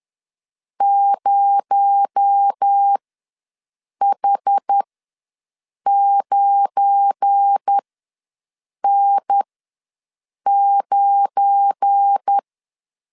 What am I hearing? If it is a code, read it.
Morse code, heard as a keyed tone, about 10 wpm, spaced as for 6 wpm, 800 Hz. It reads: 0H9N9
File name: LEZIONI / 7 GF9Z / 7 GF9 ws 10wpm 4034